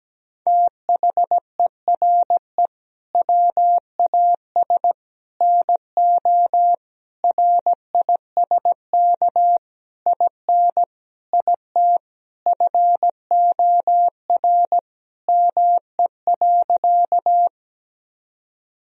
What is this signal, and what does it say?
Morse code, heard as a keyed tone, about 17 wpm, 700 Hz. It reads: THERE WAS NO RISK IN IT FOR ME.
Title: Sentences from Top 1000 Words - Review Part 4 - 17wpm